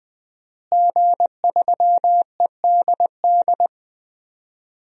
Morse code, heard as a keyed tone, about 21 wpm, 700 Hz. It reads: G3EDD
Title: DXCC-100 Call Signs - 5 Characters - Repeated 3x - 20wpm